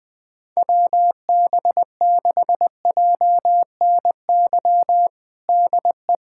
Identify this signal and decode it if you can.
Morse code, heard as a keyed tone, about 19 wpm, 700 Hz. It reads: WB6JNY DE
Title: USA Callsign DE Callsign 20wpm